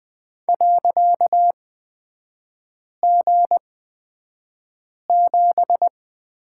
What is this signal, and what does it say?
Morse code, heard as a keyed tone, about 20 wpm, spaced as for 5 wpm, 700 Hz. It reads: .G7
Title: 255 Lesson 41e - Period - Three Characters 20wpm